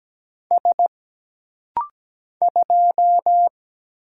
Morse code, heard as a keyed tone, about 17 wpm, 700 Hz. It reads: S 2
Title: Single Letter-Number Alt - Rapid-Fire 17wpm